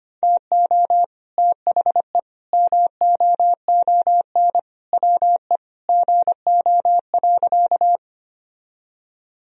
Morse code, heard as a keyed tone, about 25 wpm, 700 Hz. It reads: TO THE MOON WE GO.
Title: Sentences from Top 1000 Words - Review Part 3 - 25wpm